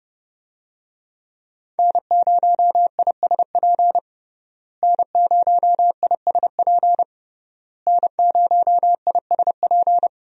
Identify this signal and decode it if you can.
Morse code, heard as a keyed tone, about 30 wpm, 700 Hz. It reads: N0ISP N0ISP N0ISP
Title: USA Call Signs - 5 Characters - Repeated 3x - 30wpm